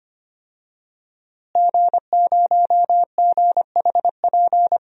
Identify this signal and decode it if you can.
Morse code, heard as a keyed tone, about 25 wpm, 700 Hz. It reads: G0GHP
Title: DXCC-100 Call Signs - 5 Characters - Repeated 2x - 25wpm